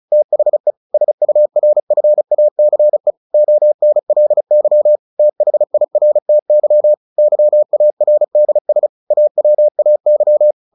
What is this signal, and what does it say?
Morse code, heard as a keyed tone, about 34 wpm, 600 Hz. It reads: THE SURFACE ONLY THIRTY YARDS AWAY
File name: the old man and the sea 2 35wpm